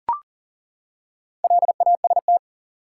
Morse code, heard as a keyed tone, about 39 wpm, 700 Hz. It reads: LAST
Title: Four Letter Words - 40wpm